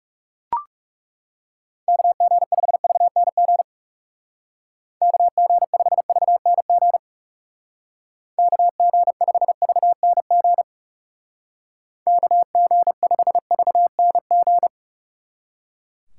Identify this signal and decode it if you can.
Morse code, heard as a keyed tone, about 38 wpm, 700 Hz. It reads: KG5VNG KG5VNG KG5VNG KG5VNG